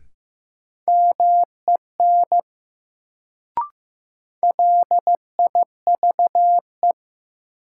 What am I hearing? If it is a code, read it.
Morse code, heard as a keyed tone, about 15 wpm, 700 Hz. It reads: MEN LIVE
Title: Sentences from Top 200 Words - New Word Drill - 15wpm